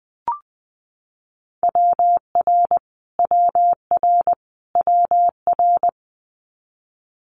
Morse code, heard as a keyed tone, about 20 wpm, 700 Hz. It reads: WR WR WR
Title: Multi-letter Phonemes Repeated 3x 20wpm